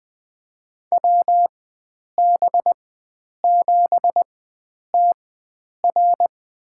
Morse code, heard as a keyed tone, about 20 wpm, spaced as for 9 wpm, 700 Hz. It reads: WB7TR
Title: USA Call Signs - 4x Character Spacing - 20wpm